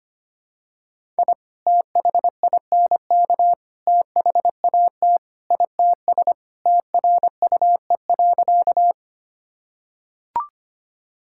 Morse code, heard as a keyed tone, about 25 wpm, 700 Hz. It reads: I THINK THAT ITS TRUE.
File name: practice / 25wpm / Sentences with 2-Word Phrases - 25wpm